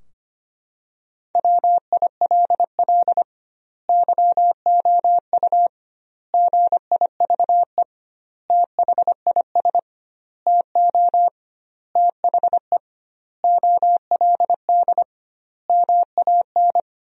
Morse code, heard as a keyed tone, about 25 wpm, 700 Hz. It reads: WILL YOU GIVE THIS TO THE OLD MAN
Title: 2x Word-Spacing - Sentences from Top 200 Words - 25wpm